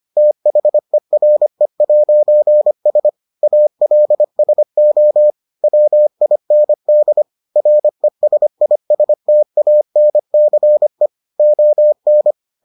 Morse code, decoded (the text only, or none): THERE'S ALSO WIND RESISTANCE ON